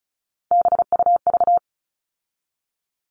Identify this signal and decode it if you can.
Morse code, heard as a keyed tone, about 34 wpm, 700 Hz. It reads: BUV
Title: All Cut Numbers - 3-Digit 35wpm